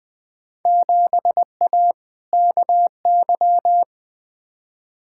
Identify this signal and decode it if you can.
Morse code, heard as a keyed tone, about 20 wpm, 700 Hz. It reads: 7A KY
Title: Field Day Exchanges Encore - 20wpm